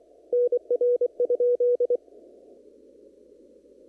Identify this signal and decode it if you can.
Morse code, heard as a keyed tone, about 24 wpm, 470 Hz. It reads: NR?
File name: CW / kt5j wpx cw-op-k5pi